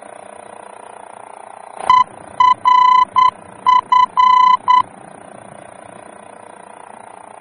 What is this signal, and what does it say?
Morse code, heard as a keyed tone, about 9 wpm, 1000 Hz. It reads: ERF
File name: germany / mx / ERF-D-VOR